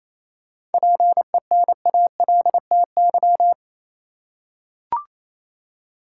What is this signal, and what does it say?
Morse code, heard as a keyed tone, about 28 wpm, 700 Hz. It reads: PENALTY
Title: Seven Letter Words - 28wpm